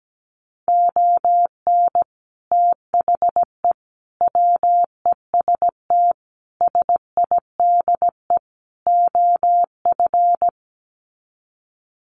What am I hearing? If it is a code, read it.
Morse code, heard as a keyed tone, about 17 wpm, 700 Hz. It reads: ON THE WEST SIDE OF
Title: Sets of 5 Words Encore2 17wpm